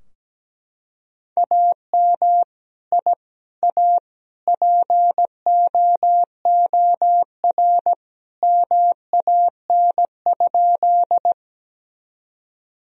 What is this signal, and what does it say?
Morse code, heard as a keyed tone, about 17 wpm, 700 Hz. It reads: AM I A POOR MAN?
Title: Sentences from Top 1000 Words - Review Part 2 - 17wpm